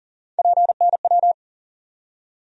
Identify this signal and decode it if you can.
Morse code, heard as a keyed tone, about 40 wpm, 700 Hz. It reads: PNW
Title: USA Call Sign Postfixes - 3 Letters 40wpm